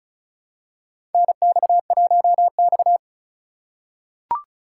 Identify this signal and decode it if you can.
Morse code, heard as a keyed tone, about 36 wpm, 700 Hz. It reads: NX1X
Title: USA Call Signs - 4 Characters - 4x Character Spacing - 35wpm